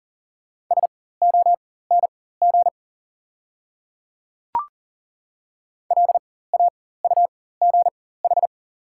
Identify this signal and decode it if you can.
Morse code, heard as a keyed tone, about 40 wpm, spaced as for 19 wpm, 690 Hz. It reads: SONG LAUGH
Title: 4x Character Spacing - Top 500 Words - 40wpm